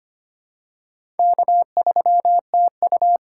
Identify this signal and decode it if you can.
Morse code, heard as a keyed tone, about 25 wpm, 700 Hz. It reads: K3TU
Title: USA Call Signs - 4 Characters - 4x Character Spacing - 25wpm